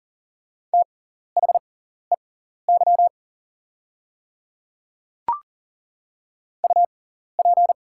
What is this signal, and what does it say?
Morse code, heard as a keyed tone, about 40 wpm, spaced as for 14 wpm, 700 Hz. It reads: THEY UP
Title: 6x Character Spacing - Top 500 Words - 40wpm